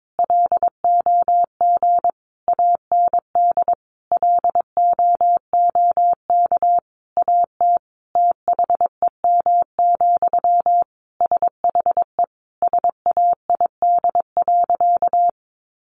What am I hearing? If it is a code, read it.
Morse code, heard as a keyed tone, about 22 wpm, 700 Hz. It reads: LOG AND LOOK AT THEM, SHE SAID.
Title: Sentences from Top 1400 Words 22wpm